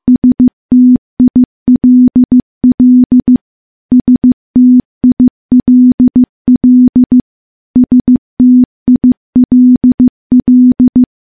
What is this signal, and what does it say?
Morse code, heard as a keyed tone, about 15 wpm, 250 Hz. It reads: STILL STILL STILL
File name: morse / morse-training-15WPM